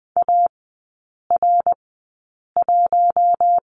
Morse code, heard as a keyed tone, about 20 wpm, spaced as for 8 wpm, 690 Hz. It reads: AR1
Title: 060 Lesson 11e - Letter h - Three Characters 20wpm